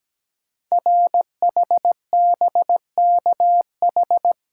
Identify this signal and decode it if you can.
Morse code, heard as a keyed tone, about 17 wpm, 700 Hz. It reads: RHBKH
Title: Code Groups - 5 Letters - 17wpm